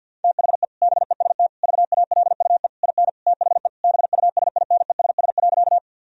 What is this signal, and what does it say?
Morse code, heard as a keyed tone, about 50 wpm, 700 Hz. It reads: THE BEST VALUE IN THE BUSINESS.
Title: Sentences from Top 1000 Words - Review Part 3 - 50wpm